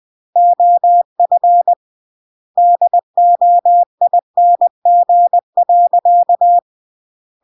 Morse code, heard as a keyed tone, about 20 wpm, 700 Hz. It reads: OF DOING.
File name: CW / jodorowsky / 82 commandments 20-20